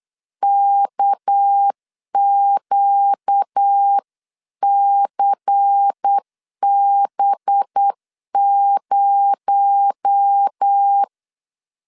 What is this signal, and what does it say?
Morse code, heard as a keyed tone, about 8 wpm, 800 Hz. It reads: KQCB0
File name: CW / DET / 1 6 1500C S B 8wpm 5142